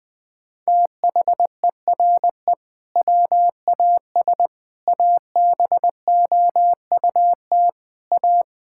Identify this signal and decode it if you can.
Morse code, heard as a keyed tone, about 20 wpm, 700 Hz. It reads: THERE WAS ABOUT A
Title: Sentences from Top 1000 Words - Review Part 5 - 20wpm